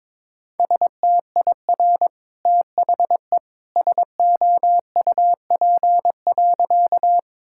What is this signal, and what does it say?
Morse code, heard as a keyed tone, about 21 wpm, 690 Hz. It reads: STIR THE SOUP.
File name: practice / 22wpm / Easy Sentences with Hints - 22wpm